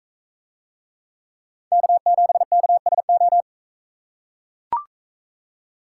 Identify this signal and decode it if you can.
Morse code, heard as a keyed tone, about 40 wpm, 700 Hz. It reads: K7KSO